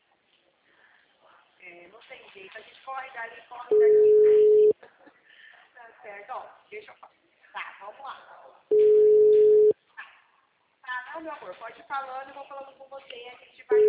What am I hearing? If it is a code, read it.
Ringback tone (Europe, 425 Hz), with speech.